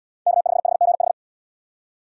Morse code, heard as a keyed tone, about 75 wpm, 700 Hz. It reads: K5HGH